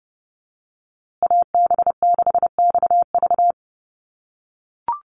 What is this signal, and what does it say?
Morse code, heard as a keyed tone, about 30 wpm, 700 Hz. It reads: AB6XV
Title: USA Call Signs Repeated 2x 30wpm